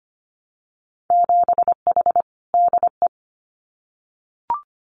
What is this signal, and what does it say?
Morse code, heard as a keyed tone, about 25 wpm, 700 Hz. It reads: 7H DE
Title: Winter Field Day Exchanges Encore 25wpm